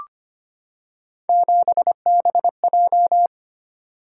Morse code, heard as a keyed tone, about 25 wpm, 700 Hz. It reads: 7BJ
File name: Three Letter-Number Combinations - 25wpm